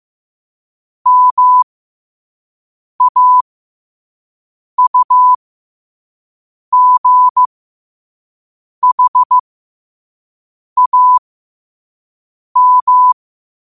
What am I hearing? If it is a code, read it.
Morse code, heard as a keyed tone, about 15 wpm, spaced as for 5 wpm, 1000 Hz. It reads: MAUGHAM
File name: qotdfiles / 5wpm